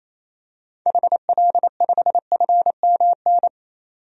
Morse code, heard as a keyed tone, about 28 wpm, 700 Hz. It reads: HL5FMN
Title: DXCC-100 Call Signs - 6 Characters - 28wpm